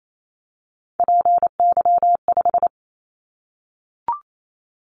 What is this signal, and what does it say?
Morse code, heard as a keyed tone, about 28 wpm, 700 Hz. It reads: PY5
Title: Callsign Prefixes for Top 50 DXCC - 28wpm